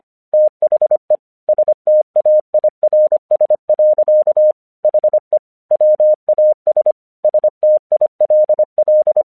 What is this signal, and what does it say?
Morse code, heard as a keyed tone, about 25 wpm, 610 Hz. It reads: THE STAIRS. HE WAS STILL
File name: 1 English 25wpm / AQC36